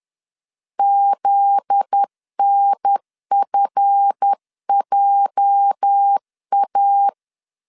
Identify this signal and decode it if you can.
Morse code, heard as a keyed tone, about 10 wpm, 800 Hz. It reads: ZNFJA
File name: LEZIONI / 8 YJ45 / 8 YJ 10wpm 2651